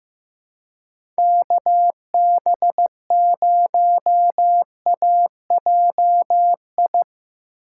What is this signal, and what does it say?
Morse code, heard as a keyed tone, about 16 wpm, 700 Hz. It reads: KB0AJI
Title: USA Call Signs - 6 Characters - 3x Character Spacing - 15wpm